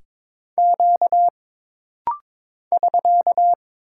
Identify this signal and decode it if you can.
Morse code, heard as a keyed tone, about 22 wpm, 700 Hz. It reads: Q <SK>
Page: This rapid-fire timing is geared to push your ICR (Instant Character Recognition) skills.